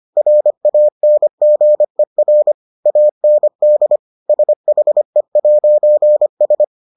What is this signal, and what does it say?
Morse code, heard as a keyed tone, about 25 wpm, 600 Hz. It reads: RANGER AND SHE'S